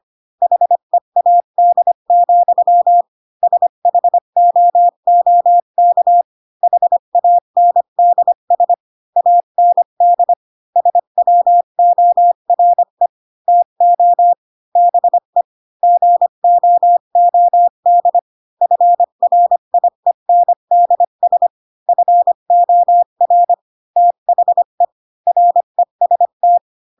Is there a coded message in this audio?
Yes: Morse code from 0.4 to 26.6 s, heard as a keyed tone, about 25 wpm, 700 Hz. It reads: HEAD, SHOOK HANDS AND SWORE TO BE GOOD FRIENDS FOR THE REST